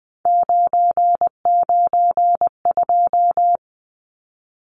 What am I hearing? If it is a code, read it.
Morse code, heard as a keyed tone, about 20 wpm, 700 Hz. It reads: 992
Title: Numbers - 3-Digit 20wpm